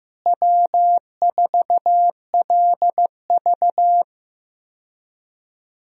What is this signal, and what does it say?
Morse code, heard as a keyed tone, about 16 wpm, 700 Hz. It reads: W4LV
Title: USA Call Signs - 4 Characters - Repeated 2x - 15wpm